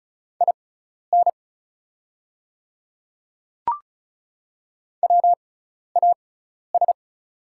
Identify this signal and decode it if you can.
Morse code, heard as a keyed tone, about 36 wpm, spaced as for 12 wpm, 700 Hz. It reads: IN WAS
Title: 6x Character Spacing - Top 100 Words - 35wpm